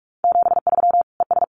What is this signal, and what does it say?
Morse code, heard as a keyed tone, about 44 wpm, 700 Hz. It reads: 73 ES
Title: Common QSO Phrases 45wpm